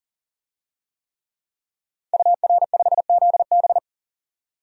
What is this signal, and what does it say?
Morse code, heard as a keyed tone, about 38 wpm, 700 Hz. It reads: UR5ZB
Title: DXCC-100 Call Signs - 5 Characters - 40wpm